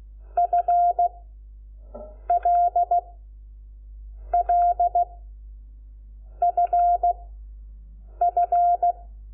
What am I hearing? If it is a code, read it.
Morse code, heard as a keyed tone, about 16 wpm, spaced as for 6 wpm, 700 Hz. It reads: FLLFF
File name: Lecciones / CLASE Nº5